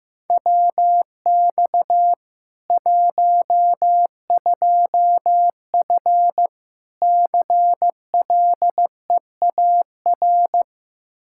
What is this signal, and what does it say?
Morse code, heard as a keyed tone, about 15 wpm, 700 Hz. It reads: WX 12F CLEAR
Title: Practice QSO - Exchange 2 - Info - 15wpm